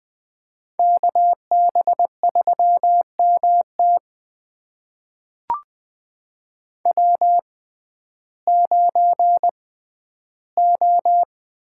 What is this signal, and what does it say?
Morse code, heard as a keyed tone, about 21 wpm, spaced as for 12 wpm, 700 Hz. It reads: KB3MT W9O
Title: USA Call Signs - 6x Character Spacing - 20wpm